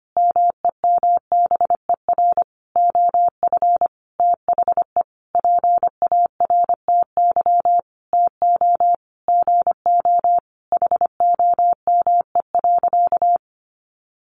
Morse code, heard as a keyed tone, about 25 wpm, 700 Hz. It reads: MEMBER OF THE PARTY TO GO HOME.
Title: Sentences from Top 1400 Words 25wpm